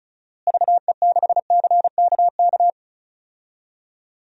Morse code, heard as a keyed tone, about 36 wpm, 700 Hz. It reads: VE6CKK
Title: DXCC-100 Call Signs - 6 Characters - Repeated 3x - 35wpm